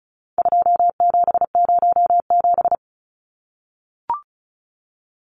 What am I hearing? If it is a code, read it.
Morse code, heard as a keyed tone, about 34 wpm, 700 Hz. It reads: 2707
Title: Numbers - 4-Digit 35wpm